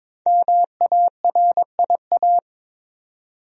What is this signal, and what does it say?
Morse code, heard as a keyed tone, about 21 wpm, 700 Hz. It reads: MARIA
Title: USA Common Names - 22wpm